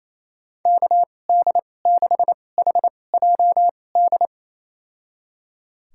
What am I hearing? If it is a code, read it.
Morse code, heard as a keyed tone, about 28 wpm, spaced as for 20 wpm, 700 Hz. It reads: KD6HJD
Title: USA Call Signs - 6 Characters - 2x Character Spacing - 28wpm